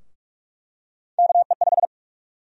Morse code, heard as a keyed tone, about 42 wpm, 700 Hz. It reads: KE5
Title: USA Call Sign Prefixes - Repeated 2x - 45wpm